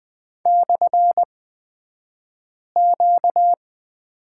Morse code, heard as a keyed tone, about 20 wpm, spaced as for 5 wpm, 700 Hz. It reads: /Q
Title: 235 Lesson 38d - Letter q - Two Characters 20wpm